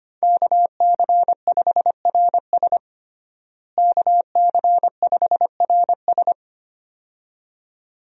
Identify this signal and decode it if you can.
Morse code, heard as a keyed tone, about 25 wpm, 700 Hz. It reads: KC5RS KC5RS
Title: USA Call Signs - 5 Characters - Repeated 2x - 25wpm